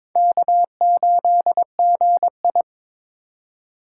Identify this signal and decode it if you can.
Morse code, heard as a keyed tone, about 22 wpm, 700 Hz. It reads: K8GI
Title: USA Call Signs - 4 Characters - 2x Character Spacing - 22wpm